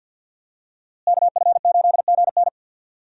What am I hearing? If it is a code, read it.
Morse code, heard as a keyed tone, about 49 wpm, 700 Hz. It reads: KF8GN
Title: USA Call Signs - 5 Characters - Repeated 2x - 50wpm